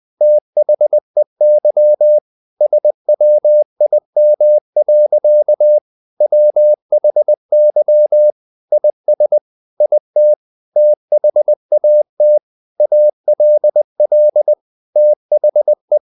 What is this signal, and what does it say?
Morse code, heard as a keyed tone, about 20 wpm, 600 Hz. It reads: THEY SWIM. WHY IS IT THAT ALL THE
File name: the old man and the sea part 2 20wpm